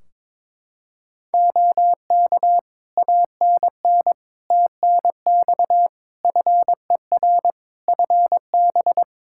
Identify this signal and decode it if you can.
Morse code, heard as a keyed tone, about 22 wpm, 700 Hz. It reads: OK ANN TNX FER FB
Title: Practice QSO - Exchange 4 - Ending - 22wpm